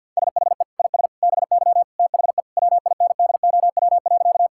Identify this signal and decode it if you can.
Morse code, heard as a keyed tone, about 51 wpm, 700 Hz. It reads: SHE IS BY THE WINDOW.
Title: Sentences from Top 1600 Words 50wpm